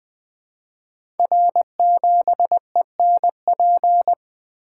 Morse code, heard as a keyed tone, about 20 wpm, 700 Hz. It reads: R7ENP
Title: Code Groups - 5 Letter:Numbers - 20wpm